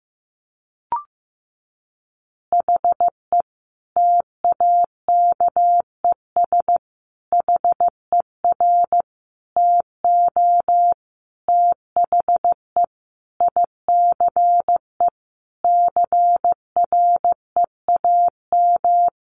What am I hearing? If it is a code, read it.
Morse code, heard as a keyed tone, about 15 wpm, 700 Hz. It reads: HE TAKES HER TO THE ICE CREAM
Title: Easy Phrases 15wpm